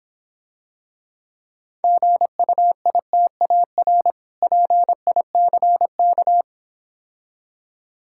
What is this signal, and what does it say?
Morse code, heard as a keyed tone, about 26 wpm, 700 Hz. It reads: GUITAR PICK